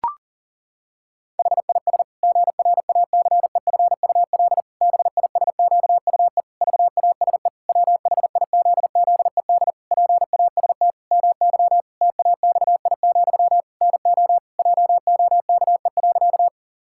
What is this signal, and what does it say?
Morse code, heard as a keyed tone, about 40 wpm, 700 Hz. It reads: HIS GRACEFUL BISQUE VASE WHIZZED PAST MY TAXI, NO JOKE.
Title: Pangram Challenge v2 - 40wpm